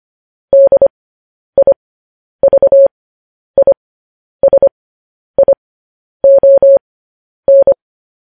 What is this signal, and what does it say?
Morse code, heard as a keyed tone, about 25 wpm, spaced as for 10 wpm, 550 Hz. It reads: DIVISION
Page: News Headlines in Morse Code at 10 WPM